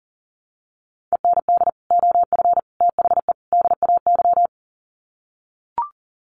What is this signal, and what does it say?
Morse code, heard as a keyed tone, about 40 wpm, 700 Hz. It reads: END OF THE DAY
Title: Sets of 4 Words 40wpm